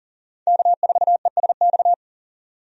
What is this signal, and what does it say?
Morse code, heard as a keyed tone, about 40 wpm, 700 Hz. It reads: K4ESX
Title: USA Call Signs - 5 Characters - 40wpm